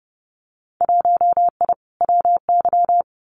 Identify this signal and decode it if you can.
Morse code, heard as a keyed tone, about 32 wpm, 700 Hz. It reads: 1I WY
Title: Winter Field Day Exchanges Encore 30wpm